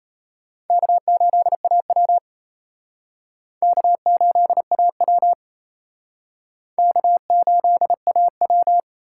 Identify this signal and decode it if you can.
Morse code, heard as a keyed tone, about 34 wpm, 700 Hz. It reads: K8AW K8AW K8AW
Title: Speed Racing - USA Call Signs 28wpm